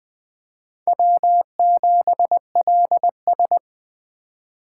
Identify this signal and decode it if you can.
Morse code, heard as a keyed tone, about 20 wpm, 700 Hz. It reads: W7LS
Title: USA Call Signs - 4 Characters - 3x Character Spacing - 20wpm